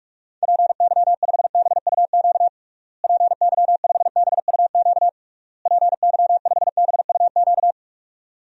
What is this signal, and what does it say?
Morse code, heard as a keyed tone, about 46 wpm, 700 Hz. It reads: PY5BUQ PY5BUQ PY5BUQ
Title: DXCC-100 Call Signs - 6 Characters - Repeated 3x - 45wpm